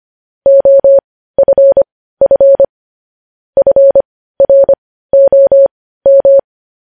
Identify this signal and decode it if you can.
Morse code, heard as a keyed tone, about 25 wpm, spaced as for 15 wpm, 550 Hz. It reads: OFF FROM